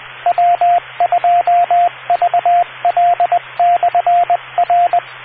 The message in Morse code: W2VL/R